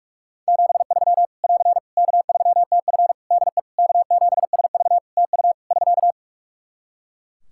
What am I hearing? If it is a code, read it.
Morse code, heard as a keyed tone, about 46 wpm, 700 Hz. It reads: 73 <AR> K3TF DE K7SV TU <SK>
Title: Practice QSO - Exchange 4 - Ending - 45wpm